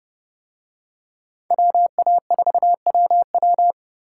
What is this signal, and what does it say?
Morse code, heard as a keyed tone, about 28 wpm, 700 Hz. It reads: WA4WW